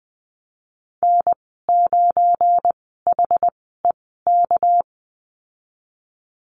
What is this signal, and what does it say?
Morse code, heard as a keyed tone, about 20 wpm, spaced as for 15 wpm, 700 Hz. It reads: N9HEK
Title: 238 Lesson 38g - Letter q - Call Signs 20wpm